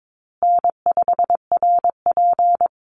Morse code, heard as a keyed tone, about 22 wpm, 700 Hz. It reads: N5RP
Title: Easy Call Signs Repeated 3x - 22wpm